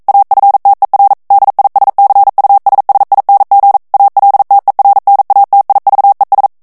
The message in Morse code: AFTER DISCUSSING ALTERNATIVES